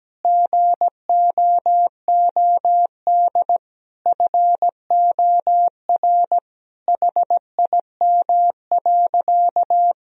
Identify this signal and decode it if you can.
Morse code, heard as a keyed tone, about 17 wpm, 700 Hz. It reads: GOOD FOR HIM.
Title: Sentences from Top 1000 Words - Review Part 1 - 17wpm